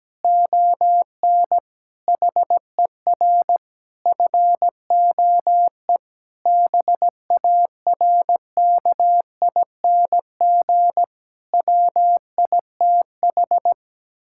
Morse code, heard as a keyed tone, about 17 wpm, 700 Hz. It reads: ON HER FOE BARKING WITH
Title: Pangram Challenge v2 - 17wpm